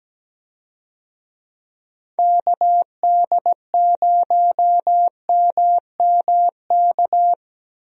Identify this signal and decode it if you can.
Morse code, heard as a keyed tone, about 17 wpm, 700 Hz. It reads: KD0MMK
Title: USA Call Signs - 6 Characters - Repeated 3x - 17wpm